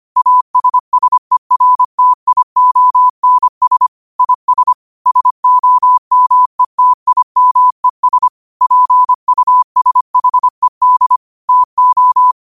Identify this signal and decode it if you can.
Morse code, heard as a keyed tone, about 25 wpm, 1000 Hz. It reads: ASSERTIONS IS SOMETIMES PUSHED TO